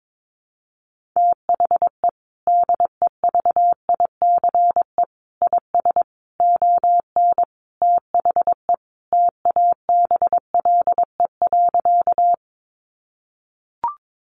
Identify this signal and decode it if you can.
Morse code, heard as a keyed tone, about 22 wpm, 700 Hz. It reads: THE DEVICE IS ON THE TABLE.
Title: Sentences from Top 1500 Words 22wpm